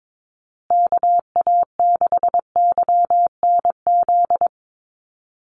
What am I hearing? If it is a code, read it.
Morse code, heard as a keyed tone, about 22 wpm, 700 Hz. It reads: KA6YNZ
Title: USA Call Signs Repeated 2x 22wpm